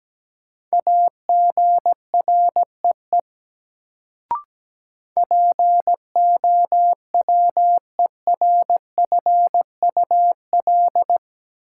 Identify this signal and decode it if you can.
Morse code, heard as a keyed tone, about 17 wpm, 700 Hz. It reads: AGREE POWERFUL
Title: Sentences from Top 1000 Words - Review Part 6 - New Word Drill - 17wpm